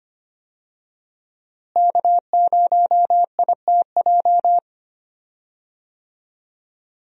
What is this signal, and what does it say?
Morse code, heard as a keyed tone, about 25 wpm, 700 Hz. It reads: K0ITJ
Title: USA Call Signs - 5 Characters - Repeated 3x - 25wpm